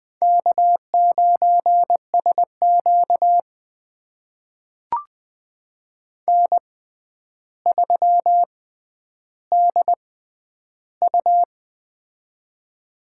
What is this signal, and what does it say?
Morse code, heard as a keyed tone, about 20 wpm, spaced as for 10 wpm, 700 Hz. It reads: K9SQ N3DU
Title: USA Call Signs - 4 Characters - 6x Character Spacing - 20wpm